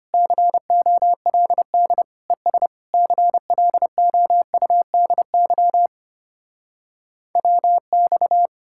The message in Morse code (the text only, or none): COLD ES CLOUDY WX